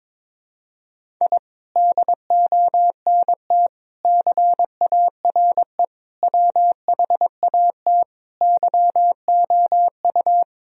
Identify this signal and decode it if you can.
Morse code, heard as a keyed tone, about 22 wpm, 700 Hz. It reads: I DONT CARE WHAT YOU
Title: Sentences from Top 1000 Words - Review Part 1 - 22wpm